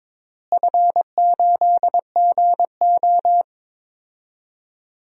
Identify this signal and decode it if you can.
Morse code, heard as a keyed tone, about 22 wpm, 700 Hz. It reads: F8GO
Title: DXCC-100 Call Signs - 4 Characters - Repeated 3x - 22wpm